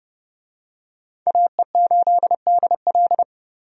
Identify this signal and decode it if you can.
Morse code, heard as a keyed tone, about 30 wpm, 700 Hz. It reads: AE8DL
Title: USA Call Signs - 5 Characters - 30wpm